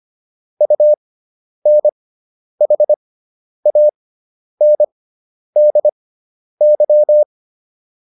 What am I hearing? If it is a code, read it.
Morse code, heard as a keyed tone, about 25 wpm, spaced as for 10 wpm, 600 Hz. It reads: UNHANDY